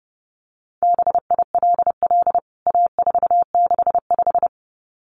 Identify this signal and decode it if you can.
Morse code, heard as a keyed tone, about 30 wpm, 700 Hz. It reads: BILL A465
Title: CWT Exchange 30wpm